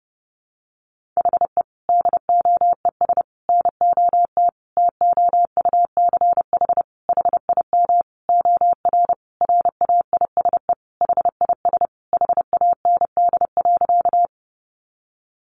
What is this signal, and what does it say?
Morse code, heard as a keyed tone, about 30 wpm, 700 Hz. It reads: HE DOES NOT TOUCH HIM OR RAISE HIS HAND.
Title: Sentences from Top 1400 Words 30wpm